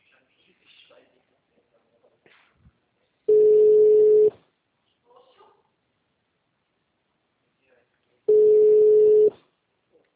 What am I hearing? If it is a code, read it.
Ringback tone (Europe, 425 Hz), with speech.